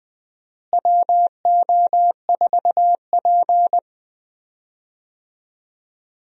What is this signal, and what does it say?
Morse code, heard as a keyed tone, about 19 wpm, 700 Hz. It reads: WO4P
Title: USA Call Signs - 4 Characters - Repeated 3x - 20wpm